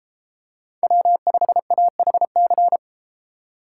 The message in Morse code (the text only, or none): W5AHC